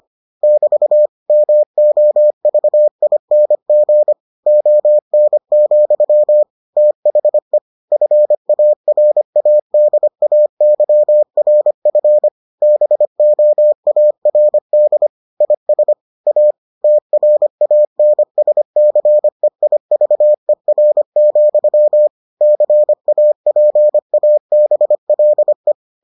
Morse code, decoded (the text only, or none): <BT> MOVING ON, THE FARADAYRF BOARD IS A TRANSCEIVER, CAPABLE